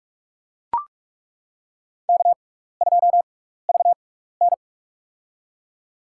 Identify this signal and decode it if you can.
Morse code, heard as a keyed tone, about 42 wpm, spaced as for 15 wpm, 700 Hz. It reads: K2VN
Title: USA Call Signs - 4 Characters - 6x Character Spacing - 45wpm